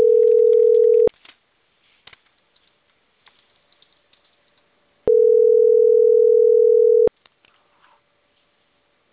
Ringback tone (North America), 440 + 480 Hz.